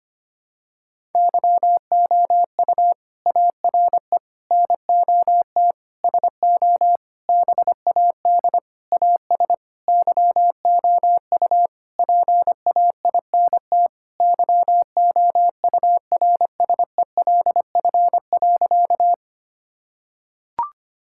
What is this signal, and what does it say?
Morse code, heard as a keyed tone, about 25 wpm, 700 Hz. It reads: YOU ARE NOT SO BAD AS YOU PAINT YOURSELF.
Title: Sentences from Top 1000 Words - Review Part 6 - 25wpm